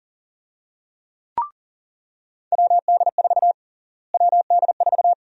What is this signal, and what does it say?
Morse code, heard as a keyed tone, about 40 wpm, 700 Hz. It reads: WD4 WD4
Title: USA Call Sign Prefixes - Repeated 2x - 40wpm